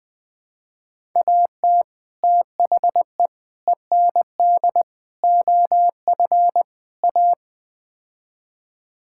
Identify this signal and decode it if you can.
Morse code, heard as a keyed tone, about 20 wpm, 700 Hz. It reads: AT THE END OF A
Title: Build A Phrase - 5-Word Phrases - 20wpm